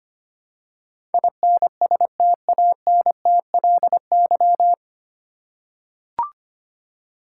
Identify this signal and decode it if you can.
Morse code, heard as a keyed tone, about 25 wpm, 700 Hz. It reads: INSTANTLY
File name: Nine Letter Words - 25wpm